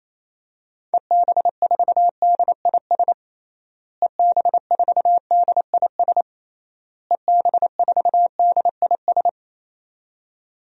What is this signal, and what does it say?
Morse code, heard as a keyed tone, about 27 wpm, 700 Hz. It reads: EB4DIS EB4DIS EB4DIS
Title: DXCC-100 Call Signs - 6 Characters - Repeated 3x - 28wpm